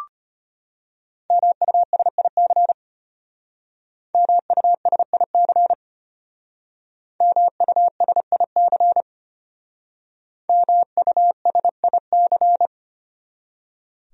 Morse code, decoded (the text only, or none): MUSIC MUSIC MUSIC MUSIC